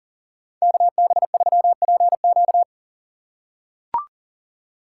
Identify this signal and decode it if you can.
Morse code, heard as a keyed tone, about 40 wpm, 700 Hz. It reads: KB3PQ
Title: USA Call Signs - 5 Characters - 3x Character Spacing - 40wpm